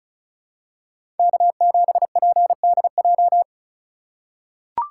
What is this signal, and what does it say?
Morse code, heard as a keyed tone, about 36 wpm, 700 Hz. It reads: K7PDJ
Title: USA Call Signs - 5 Characters - 4x Character Spacing - 35wpm